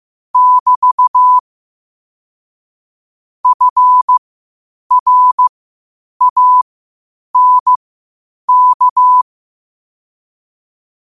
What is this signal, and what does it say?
Morse code, heard as a keyed tone, about 15 wpm, spaced as for 8 wpm, 1000 Hz. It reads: <BT> FRANK
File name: qotdfiles / 8wpm